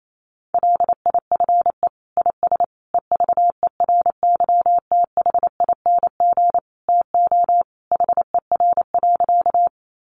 Morse code, heard as a keyed tone, about 28 wpm, 700 Hz. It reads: LIFE IS EVERYTHING TO HER.
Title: Sentences from Top 700 Words 28wpm